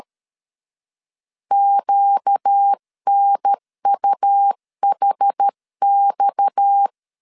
Morse code, heard as a keyed tone, about 13 wpm, 800 Hz. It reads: QNUHX
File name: CW / DET / 1 6 1500C S C 12wpm 3533